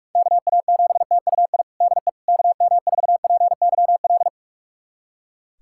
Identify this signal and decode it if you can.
Morse code, heard as a keyed tone, about 41 wpm, 690 Hz. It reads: KA7TUI DE KM4PYL
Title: USA Callsign DE Callsign 45wpm